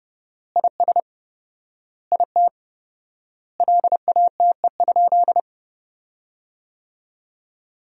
Morse code, heard as a keyed tone, about 32 wpm, 700 Hz. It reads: IS IT LATE?
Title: 4x Word-Spacing - Sentences from Top 300 Words - 30wpm